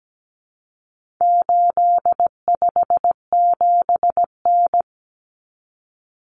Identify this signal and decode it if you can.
Morse code, heard as a keyed tone, about 17 wpm, 700 Hz. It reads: 857N
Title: Common Cut Numbers - 4-Digit 17wpm